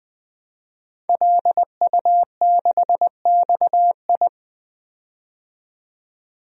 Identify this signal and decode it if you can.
Morse code, heard as a keyed tone, about 20 wpm, 700 Hz. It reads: LU6XI
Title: DXCC-100 Call Signs - 5 Characters - 20wpm